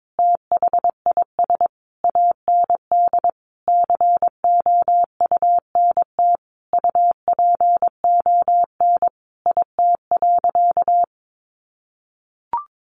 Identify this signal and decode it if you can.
Morse code, heard as a keyed tone, about 22 wpm, 700 Hz. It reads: THIS AND COUNT UPON IT.
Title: Sentences from Top 1500 Words 22wpm